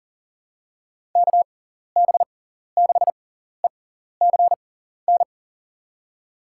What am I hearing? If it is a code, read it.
Morse code, heard as a keyed tone, about 40 wpm, spaced as for 14 wpm, 700 Hz. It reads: KB6ECN
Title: USA Call Signs - 6 Characters - 6x Character Spacing - 40wpm